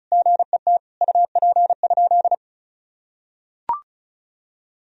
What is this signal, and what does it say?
Morse code, heard as a keyed tone, about 36 wpm, 700 Hz. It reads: GET UP?
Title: Sentences from Top 1000 Words - Review Part 1 - 35wpm